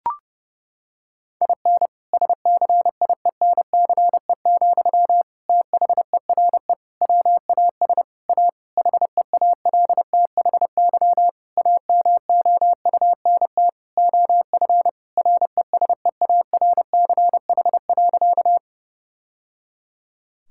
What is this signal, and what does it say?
Morse code, heard as a keyed tone, about 32 wpm, 700 Hz. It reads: IN SCIENCE, THERE WAS A HEALTHY AMOUNT OF RESEARCH.
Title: Sentences from Top 1000 Words - Review Part 5 - 30wpm